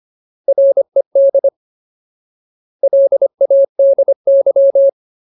Morse code, heard as a keyed tone, about 25 wpm, 550 Hz. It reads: RED LADY
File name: Ultimate Word List / 25WPM Ultimate